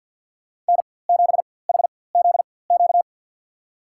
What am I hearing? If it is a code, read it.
Morse code, heard as a keyed tone, about 53 wpm, spaced as for 24 wpm, 690 Hz. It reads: N7HZQ
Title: USA Call Signs - 4x Character Spacing - 50wpm